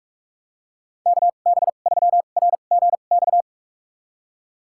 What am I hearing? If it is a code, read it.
Morse code, heard as a keyed tone, about 41 wpm, spaced as for 31 wpm, 700 Hz. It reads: KB3RGX
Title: USA Call Signs - 6 Characters - 2x Character Spacing - 45wpm